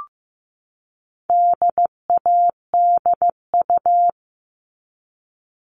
Morse code, heard as a keyed tone, about 15 wpm, 700 Hz. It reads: DADU
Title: All Cut Numbers - 4-Digit 15wpm